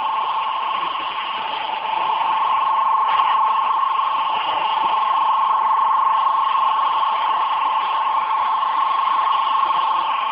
Two-tone teleprinter signal, about 200 Hz shift.